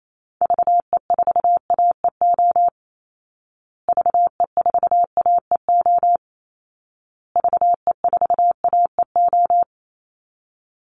Morse code, heard as a keyed tone, about 28 wpm, 700 Hz. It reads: VE4AEO VE4AEO VE4AEO
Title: Call Signs Repeated 2x - 28wpm